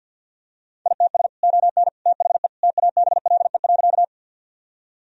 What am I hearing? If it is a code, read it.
Morse code, heard as a keyed tone, about 50 wpm, 700 Hz. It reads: ITS ON THE TABLE.
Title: Sentences from Top 1000 Words - Review Part 2 - 50wpm